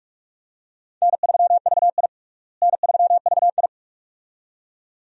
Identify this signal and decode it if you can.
Morse code, heard as a keyed tone, about 47 wpm, 700 Hz. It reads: N3VI N3VI
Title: USA Call Signs - 4 Characters - Repeated 2x - 45wpm